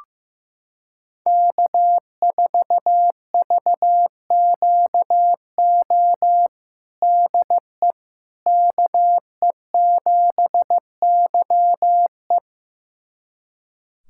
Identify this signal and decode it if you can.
Morse code, heard as a keyed tone, about 15 wpm, 700 Hz. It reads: K4VQO DE KE7YE
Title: USA Callsign DE Callsign - 5 Characters - 15wpm